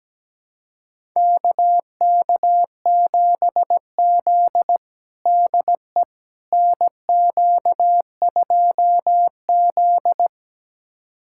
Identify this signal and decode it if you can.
Morse code, heard as a keyed tone, about 17 wpm, 700 Hz. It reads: KK7Z DE NQ2Z
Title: USA Callsign DE Callsign - 4 Characters - 17wpm